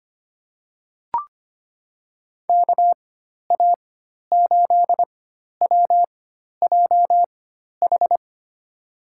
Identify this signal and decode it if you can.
Morse code, heard as a keyed tone, about 25 wpm, spaced as for 12 wpm, 700 Hz. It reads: KA8WJH
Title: USA Call Signs - 4x Character Spacing - 25wpm